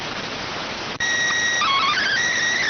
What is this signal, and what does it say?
SSTV (slow-scan television), PD 290.